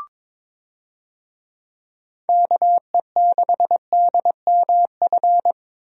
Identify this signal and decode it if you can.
Morse code, heard as a keyed tone, about 22 wpm, 700 Hz. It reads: KE6DMF
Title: USA Call Signs - 6 Characters - Repeated 3x - 22wpm